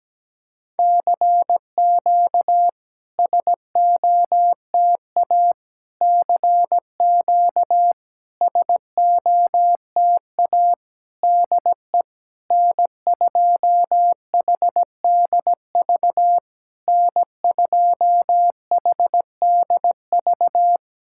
Morse code, heard as a keyed tone, about 17 wpm, 700 Hz. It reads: CQ SOTA CQ SOTA DE N2HDV N2HDV
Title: SOTA QSO Practice 17wpm